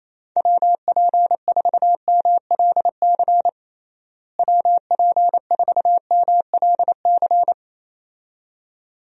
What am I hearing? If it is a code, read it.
Morse code, heard as a keyed tone, about 28 wpm, 700 Hz. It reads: WP4MLC WP4MLC
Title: USA Call Signs - 6 Characters - Repeated 2x - 28wpm